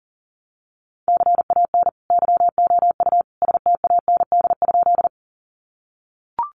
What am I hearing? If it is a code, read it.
Morse code, heard as a keyed tone, about 40 wpm, 700 Hz. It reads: CAN YOU STAND?
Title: Sentences from Top 300 Words 40wpm